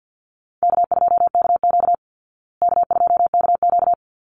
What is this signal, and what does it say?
Morse code, heard as a keyed tone, about 50 wpm, 700 Hz. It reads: K2KQ K2KQ
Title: Easy Call Signs Repeated 2x - 50wpm